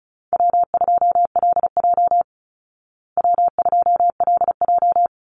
Morse code, heard as a keyed tone, about 34 wpm, 700 Hz. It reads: W2LJ W2LJ
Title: Easy Call Signs Repeated 2x - 35wpm